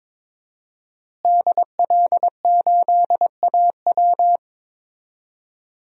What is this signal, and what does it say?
Morse code, heard as a keyed tone, about 22 wpm, 700 Hz. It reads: DL8AW
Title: DXCC-100 Call Signs - 5 Characters - Repeated 2x - 22wpm